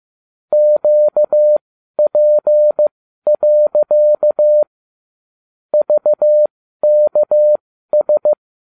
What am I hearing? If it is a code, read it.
Morse code, heard as a keyed tone, about 15 wpm, spaced as for 12 wpm, 600 Hz. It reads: QP. VKS